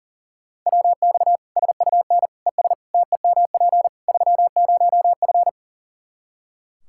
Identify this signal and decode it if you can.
Morse code, heard as a keyed tone, about 40 wpm, 700 Hz. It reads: WX SUN ES TEMP 30F
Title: Practice QSO - Exchange 2 - Info - 40wpm